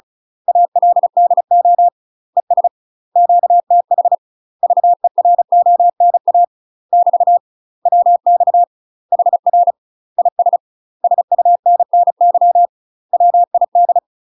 Morse code, heard as a keyed tone, about 36 wpm, 700 Hz. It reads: ALDO ES QTH VERONA <BT> WX HR IS SUNNY WID
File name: UK MASTER QSOs TRAINING / 35WPM / UK MASTER QSOs 0023